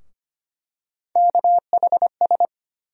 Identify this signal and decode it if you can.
Morse code, heard as a keyed tone, about 25 wpm, 700 Hz. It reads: KHS
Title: USA Call Sign Postfixes - 3 Letters 25wpm